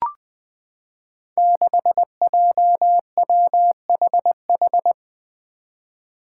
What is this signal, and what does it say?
Morse code, heard as a keyed tone, about 20 wpm, 700 Hz. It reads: 6JWHH
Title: Five Challenging Letter-Number Combinations 20wpm